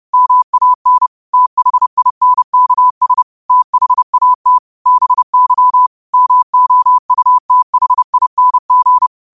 Morse code, heard as a keyed tone, about 30 wpm, 1000 Hz. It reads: MAN THINKS THAT BY MOUTHING
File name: qotdfiles / 30wpm